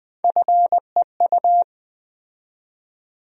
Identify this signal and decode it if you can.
Morse code, heard as a keyed tone, about 20 wpm, 700 Hz. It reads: FEU
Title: USA Call Sign Postfixes - 3 Letters 20wpm